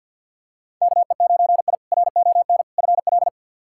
Morse code, heard as a keyed tone, about 47 wpm, 700 Hz. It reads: KE9I RON FL
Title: K1USN SST QSO Practice - 50wpm